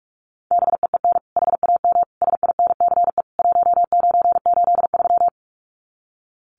Morse code, heard as a keyed tone, about 44 wpm, 700 Hz. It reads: BEEN HAM SINCE 1983
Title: Common QSO Phrases 45wpm